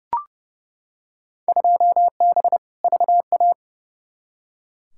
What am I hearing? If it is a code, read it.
Morse code, heard as a keyed tone, about 32 wpm, 700 Hz. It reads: 2B VA
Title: Field Day Exchanges Encore - 30wpm